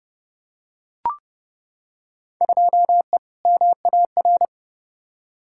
Morse code, heard as a keyed tone, about 30 wpm, 700 Hz. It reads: 2E MAR
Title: Field Day Exchanges Encore - 30wpm